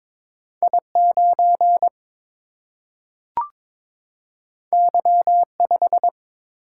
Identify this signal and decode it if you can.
Morse code, heard as a keyed tone, about 22 wpm, 700 Hz. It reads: I 9 Y 5
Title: Single-Letter Single-Number 22wpm